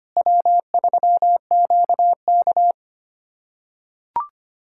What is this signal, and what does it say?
Morse code, heard as a keyed tone, about 25 wpm, 700 Hz. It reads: W3QK
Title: USA Callsign DE Callsign - 4 Characters - 25wpm